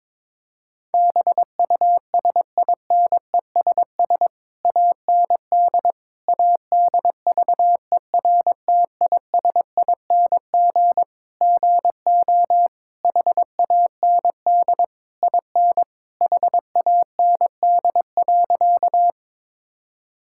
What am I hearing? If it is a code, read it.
Morse code, heard as a keyed tone, about 22 wpm, 700 Hz. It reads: BUSINESS AND ADVERTISING GO HAND IN HAND.
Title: Sentences from Top 1000 Words - Review Part 6 - 22wpm